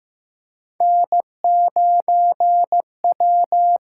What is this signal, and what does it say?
Morse code, heard as a keyed tone, about 15 wpm, 700 Hz. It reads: N9W
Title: USA Call Signs - 8x Character Spacing - 15wpm